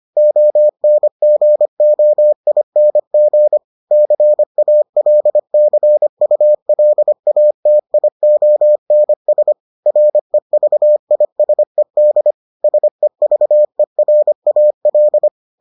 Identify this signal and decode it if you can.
Morse code, heard as a keyed tone, about 25 wpm, 600 Hz. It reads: ONGOING CALCULATIONS REVISED SEVERAL